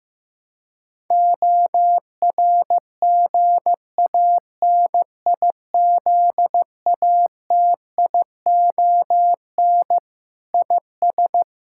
Morse code, heard as a keyed tone, about 15 wpm, 700 Hz. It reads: ORGANIZATION IS
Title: Sentences from Top 1000 Words - Review Part 5 - 15wpm